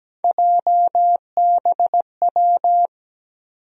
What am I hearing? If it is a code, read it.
Morse code, heard as a keyed tone, about 17 wpm, 700 Hz. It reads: JBW
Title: USA Call Sign Postfixes - 3 Letters 17wpm